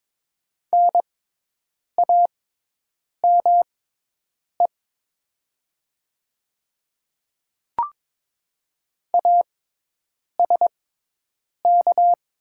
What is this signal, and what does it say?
Morse code, heard as a keyed tone, about 22 wpm, spaced as for 8 wpm, 700 Hz. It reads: NAME ASK
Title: 6x Character Spacing - Top 300 Words - 22wpm